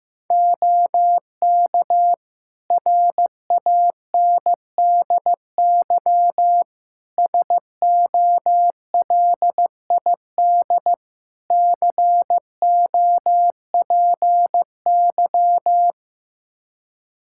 Morse code, heard as a keyed tone, about 15 wpm, 700 Hz. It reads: OK RANDY SOLID COPY
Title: Practice QSO - Exchange 3 - More Info - 15wpm